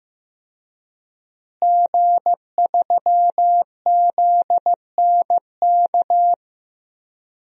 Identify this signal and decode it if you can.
Morse code, heard as a keyed tone, about 16 wpm, 700 Hz. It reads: G3ZNK
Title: DXCC-100 Call Signs - 5 Characters - Repeated 3x - 15wpm